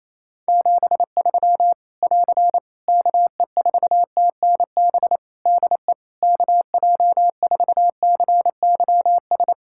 Morse code, heard as a keyed tone, about 28 wpm, 700 Hz. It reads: 73 <AR> KE4TNB DE KJ4CYS
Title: Practice QSO - Exchange 4 - Ending - 28wpm